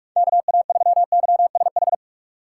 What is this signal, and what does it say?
Morse code, heard as a keyed tone, about 46 wpm, 700 Hz. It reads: KA3YSH
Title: USA Call Signs - 6 Characters - 6x Character Spacing - 45wpm